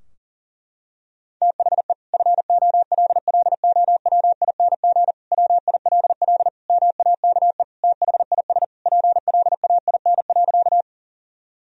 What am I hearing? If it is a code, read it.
Morse code, heard as a keyed tone, about 37 wpm, 700 Hz. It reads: THE FOLLOWING WILL MAKE THIS PLAIN.
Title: Sentences from Top 1000 Words - Review Part 4 - 40wpm